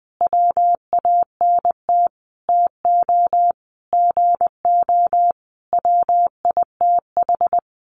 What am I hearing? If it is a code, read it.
Morse code, heard as a keyed tone, about 20 wpm, 700 Hz. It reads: WANT TO GO WITH
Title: Sentences from Top 1300 Words 20wpm